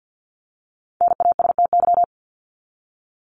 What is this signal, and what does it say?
Morse code, heard as a keyed tone, about 48 wpm, 700 Hz. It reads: NASTY
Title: Top 1751-2000 Words 50wpm